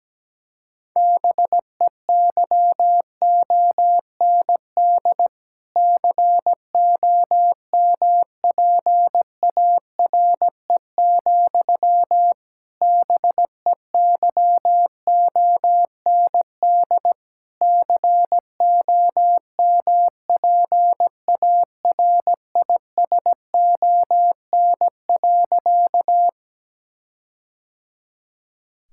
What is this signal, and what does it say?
Morse code, heard as a keyed tone, about 17 wpm, 700 Hz. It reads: BEYOND COMPARE, BEYOND COMPARISON.
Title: Sentences from Top 1600 Words 17wpm